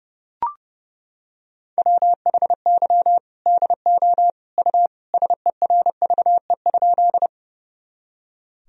Morse code, heard as a keyed tone, about 30 wpm, 700 Hz. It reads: WHY DO U SERVE?
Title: Sentences from Top 400 Words and Abbreviations 30wpm